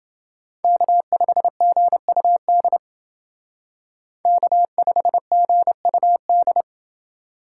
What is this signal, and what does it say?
Morse code, heard as a keyed tone, about 28 wpm, 700 Hz. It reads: K5GUD K5GUD